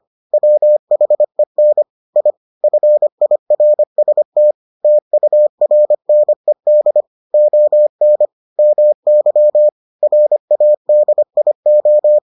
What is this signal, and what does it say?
Morse code, heard as a keyed tone, about 25 wpm, 600 Hz. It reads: WHEN I FIRST TURNED ON MY RADIO